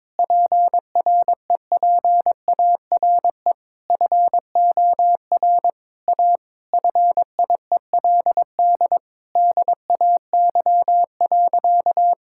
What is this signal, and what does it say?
Morse code, heard as a keyed tone, about 22 wpm, 700 Hz. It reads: PREPARE FOR A FIELD DAY.
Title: Sentences from Top 1000 Words - Review Part 6 - 22wpm